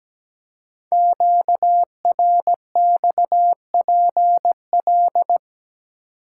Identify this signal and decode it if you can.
Morse code, heard as a keyed tone, about 17 wpm, 700 Hz. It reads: QRXPL
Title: Code Groups - 5 Letters - 17wpm